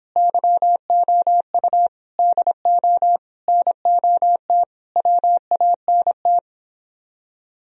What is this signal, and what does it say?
Morse code, heard as a keyed tone, about 26 wpm, 700 Hz. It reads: YOU DO NOT WANT
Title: Speed-Racing - Sets of 5 Words 22wpm